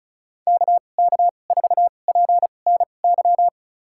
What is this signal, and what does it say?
Morse code, heard as a keyed tone, about 34 wpm, spaced as for 25 wpm, 700 Hz. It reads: KK4PNY
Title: USA Call Signs - 6 Characters - 2x Character Spacing - 35wpm